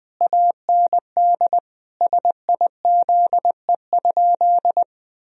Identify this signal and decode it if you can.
Morse code, heard as a keyed tone, about 20 wpm, 700 Hz. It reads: AND SIZE?
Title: Sentences from Top 1000 Words - Review Part 3 - 20wpm